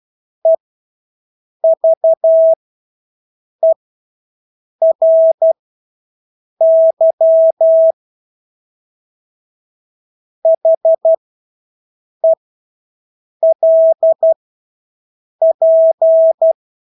Morse code, heard as a keyed tone, about 12 wpm, spaced as for 6 wpm, 650 Hz. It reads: EVERY HELP
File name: Engelske ord 12-6 wpm